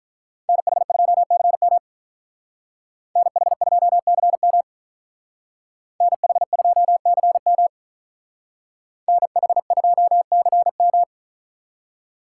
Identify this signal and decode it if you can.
Morse code, heard as a keyed tone, about 45 wpm, 700 Hz. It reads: NH2CM NH2CM NH2CM NH2CM